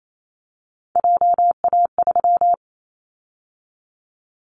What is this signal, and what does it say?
Morse code, heard as a keyed tone, about 28 wpm, 700 Hz. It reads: JA3
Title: Callsign Prefixes for Top 50 DXCC - 28wpm